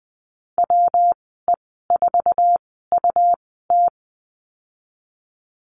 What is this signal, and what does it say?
Morse code, heard as a keyed tone, about 20 wpm, spaced as for 15 wpm, 700 Hz. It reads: WE4UT
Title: 128 Lesson 21g - Number 6 - Call Signs 20wpm